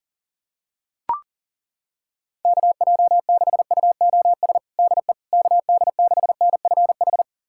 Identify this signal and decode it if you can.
Morse code, heard as a keyed tone, about 40 wpm, 700 Hz. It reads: KJ6UOS DE KD6NFH
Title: USA Callsign DE Callsign - 6 Characters - 40wpm